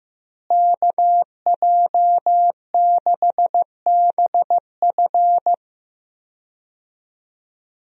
Morse code, heard as a keyed tone, about 15 wpm, 700 Hz. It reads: KJ6BF
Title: USA Call Signs - 5 Characters - Repeated 2x - 15wpm